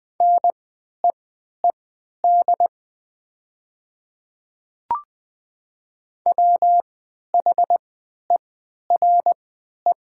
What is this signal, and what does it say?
Morse code, heard as a keyed tone, about 20 wpm, spaced as for 11 wpm, 690 Hz. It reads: NEED WHERE
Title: 3x Character Spacing - Top 200 Words - 20wpm